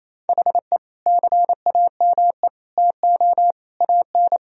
Morse code, heard as a keyed tone, about 28 wpm, 700 Hz. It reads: HE CAME TO AN
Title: Sentences from Top 1000 Words - Review Part 3 - 28wpm